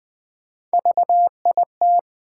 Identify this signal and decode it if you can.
Morse code, heard as a keyed tone, about 20 wpm, 700 Hz. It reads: VIT
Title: Build A Word - 1 Letter At A Time - 5-Letter Words - 20wpm